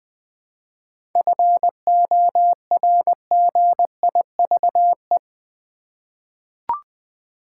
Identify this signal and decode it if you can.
Morse code, heard as a keyed tone, about 20 wpm, 690 Hz. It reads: FORGIVE
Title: 7 Letter Words - First 3 Letters Given - 20wpm